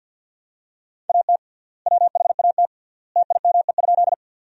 Morse code, heard as a keyed tone, about 52 wpm, 700 Hz. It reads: AT WHAT TIME?
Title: 3x Word-Spacing - Sentences from Top 100 Words - 50wpm